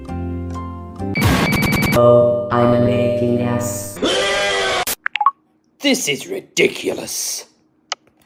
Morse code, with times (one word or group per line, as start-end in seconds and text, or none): none